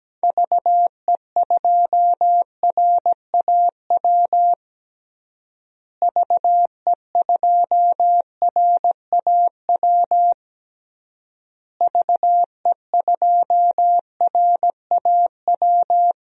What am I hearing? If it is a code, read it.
Morse code, heard as a keyed tone, about 17 wpm, 700 Hz. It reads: VE2RAW VE2RAW VE2RAW
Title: DXCC-100 Call Signs - 6 Characters - Repeated 3x - 17wpm